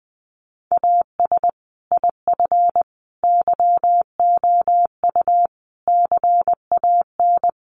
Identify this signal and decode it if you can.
Morse code, heard as a keyed tone, about 19 wpm, 700 Hz. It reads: AS IF YOU CAN
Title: Sentences from Top 1300 Words 20wpm